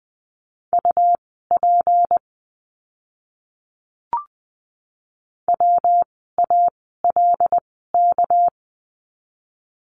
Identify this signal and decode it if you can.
Morse code, heard as a keyed tone, about 20 wpm, spaced as for 14 wpm, 700 Hz. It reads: UP WALK
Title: 199 Lesson 32f - Letter k - Words 20wpm